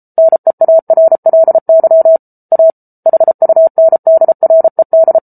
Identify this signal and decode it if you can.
Morse code, heard as a keyed tone, about 34 wpm, 650 Hz. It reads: NEARLY A HUNDRED